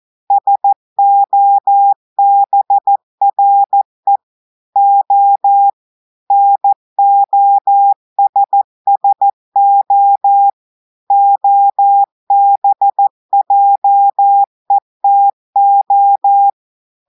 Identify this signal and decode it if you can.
Morse code, heard as a keyed tone, about 14 wpm, 800 Hz. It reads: SOBRE O NOSSO OBJETO
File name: audios / portugais / portugais texte en clair 1 vitesse 14 mots minutes